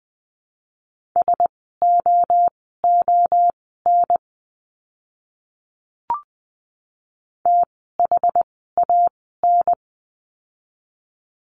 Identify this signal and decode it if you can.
Morse code, heard as a keyed tone, about 20 wpm, spaced as for 15 wpm, 700 Hz. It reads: SOON THAN
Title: 075 Lesson 13f - Letter L - Words 20wpm